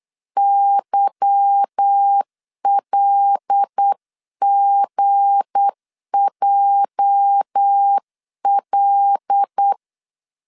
Morse code, heard as a keyed tone, about 8 wpm, 800 Hz. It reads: YLGJL
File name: CW / DET / 1 9 1500C S B 8wpm 5624